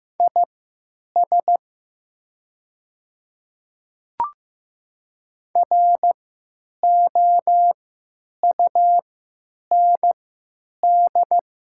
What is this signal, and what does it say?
Morse code, heard as a keyed tone, about 15 wpm, spaced as for 9 wpm, 700 Hz. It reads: IS ROUND
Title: 3x Character Spacing - Top 300 Words - 15wpm